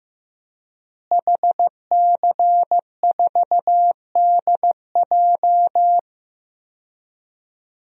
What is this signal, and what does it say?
Morse code, heard as a keyed tone, about 16 wpm, 700 Hz. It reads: HC4DJ
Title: DXCC-100 Call Signs - 5 Characters - Repeated 3x - 15wpm